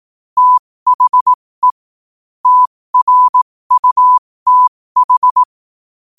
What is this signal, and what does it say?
Morse code, heard as a keyed tone, about 18 wpm, spaced as for 15 wpm, 1000 Hz. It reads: THE TRUTH
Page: New quotes every day in morse code at 15 Words per minute.